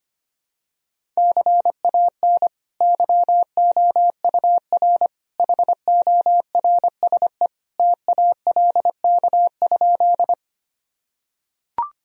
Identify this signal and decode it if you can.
Morse code, heard as a keyed tone, about 25 wpm, 700 Hz. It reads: CAN YOUR HORSE TALK?
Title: Sentences from Top 1000 Words - Review Part 1 - 25wpm